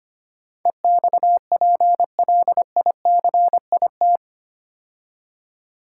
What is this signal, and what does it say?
Morse code, heard as a keyed tone, about 25 wpm, 700 Hz. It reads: EXPLICIT
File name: 8 Letter Words - First 3 Letters Given - 25wpm